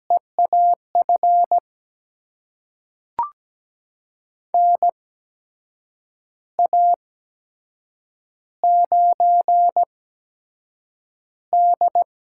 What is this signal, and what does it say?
Morse code, heard as a keyed tone, about 17 wpm, spaced as for 6 wpm, 700 Hz. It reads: E AF NA9D
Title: USA Call Signs - 8x Character Spacing - 17wpm